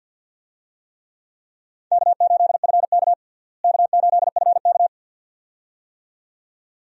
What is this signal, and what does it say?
Morse code, heard as a keyed tone, about 45 wpm, 700 Hz. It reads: K8FK K8FK
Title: USA Call Signs - 4 Characters - Repeated 2x - 50wpm